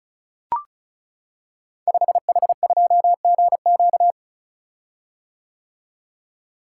Morse code, heard as a keyed tone, about 35 wpm, 700 Hz. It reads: 5H2GQ
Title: Code Groups - 5 Letter:Numbers - 35wpm